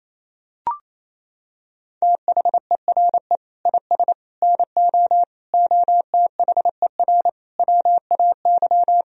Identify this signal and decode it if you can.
Morse code, heard as a keyed tone, about 27 wpm, 700 Hz. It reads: THERE IS NO OTHER WAY
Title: Sentences from Top 1000 Words - Review Part 2 - 28wpm